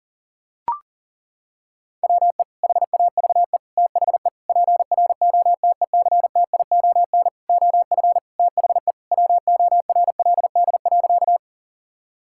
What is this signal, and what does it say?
Morse code, heard as a keyed tone, about 40 wpm, 700 Hz. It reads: WE HAVE THE PROTECTION OF THE WORLD.
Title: Sentences from Top 1600 Words 40wpm